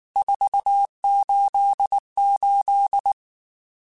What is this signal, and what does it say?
Morse code, heard as a keyed tone, about 19 wpm, 800 Hz. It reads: 488
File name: RSTs / RST-mp3s / Various RST @18